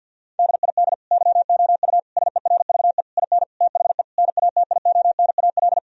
Morse code, heard as a keyed tone, about 50 wpm, 700 Hz. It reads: DID YOU SERVE IN THE NATIONAL
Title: Sentences from Top 1000 Words - Review Part 4 - 50wpm